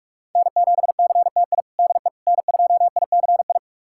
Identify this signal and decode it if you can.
Morse code, heard as a keyed tone, about 43 wpm, 700 Hz. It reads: N7CTI DE N2ICI
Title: USA Callsign DE Callsign - 5 Characters - 45wpm